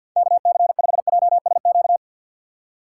Morse code, heard as a keyed tone, about 48 wpm, 700 Hz. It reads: KC5JSQ
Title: USA Call Signs - 6 Characters - 4x Character Spacing - 50wpm